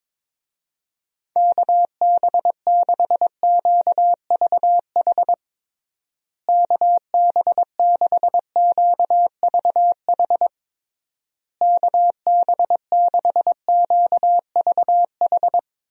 Morse code, heard as a keyed tone, about 22 wpm, 700 Hz. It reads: KB6QVH KB6QVH KB6QVH
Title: USA Call Signs - 6 Characters - Repeated 3x - 22wpm